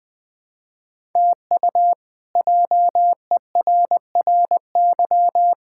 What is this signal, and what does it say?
Morse code, heard as a keyed tone, about 20 wpm, 700 Hz. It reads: TU JERRY
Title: K1USN SST QSO Practice - 20wpm